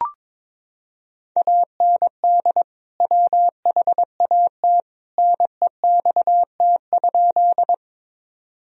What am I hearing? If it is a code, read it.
Morse code, heard as a keyed tone, about 22 wpm, 700 Hz. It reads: AND WHAT NEXT?
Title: Sentences from Top 1000 Words - Review Part 1 - 22wpm